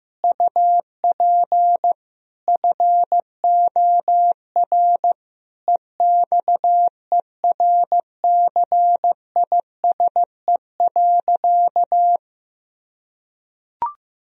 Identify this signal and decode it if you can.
Morse code, heard as a keyed tone, about 15 wpm, 700 Hz. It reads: UP FOR EXERCISE.
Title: Sentences from Top 1000 Words - Review Part 4 - 15wpm